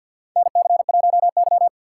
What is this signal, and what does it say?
Morse code, heard as a keyed tone, about 50 wpm, 700 Hz. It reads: NC1Y
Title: USA Call Signs - 4 Characters - 2x Character Spacing - 50wpm